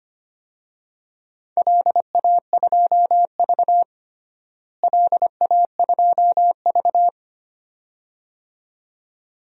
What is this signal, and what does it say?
Morse code, heard as a keyed tone, about 25 wpm, 700 Hz. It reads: LA2V LA2V
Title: DXCC-100 Call Signs - 4 Characters - Repeated 2x - 25wpm